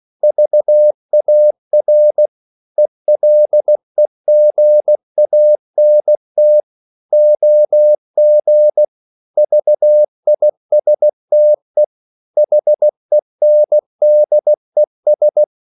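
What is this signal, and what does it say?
Morse code, heard as a keyed tone, about 16 wpm, 600 Hz. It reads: VAR ELEGANT OG VISTE HENDES
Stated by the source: Slow (QRS) Morse code at 16 wpm, from CW med Gnister.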